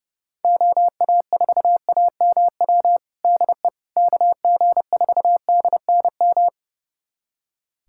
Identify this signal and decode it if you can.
Morse code, heard as a keyed tone, about 29 wpm, 700 Hz. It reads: OA4AMW DE KG4DNM
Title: USA Callsign DE Callsign 30wpm